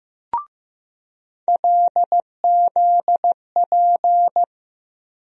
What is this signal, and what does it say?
Morse code, heard as a keyed tone, about 15 wpm, 700 Hz. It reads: LZP
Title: USA Call Sign Postfixes - 3 Letters 15wpm